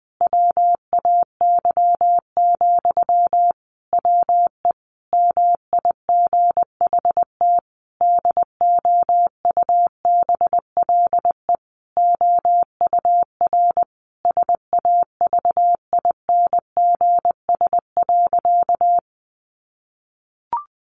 Morse code, heard as a keyed tone, about 20 wpm, 700 Hz. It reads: WAY, WE MIGHT DOUBLE OUR SAVINGS.
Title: Sentences from Top 1400 Words 20wpm